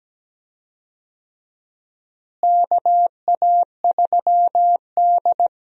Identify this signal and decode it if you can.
Morse code, heard as a keyed tone, about 17 wpm, 700 Hz. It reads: KA3D
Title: USA Call Signs - 4 Characters - Repeated 2x - 17wpm